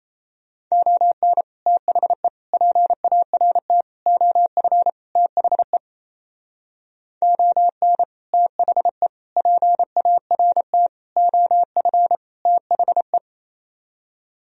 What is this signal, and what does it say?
Morse code, heard as a keyed tone, about 33 wpm, 700 Hz. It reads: ON THE PART OF THE ON THE PART OF THE
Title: Speed-Racing - Sets of 5 Words 28wpm